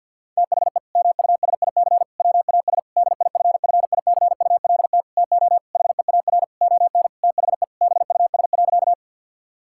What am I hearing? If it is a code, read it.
Morse code, heard as a keyed tone, about 50 wpm, 700 Hz. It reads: THE MUSIC WAS DIFFICULT TO HEAR ON THE BUS.
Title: Sentences from Top 1000 Words - Review Part 6 - 50wpm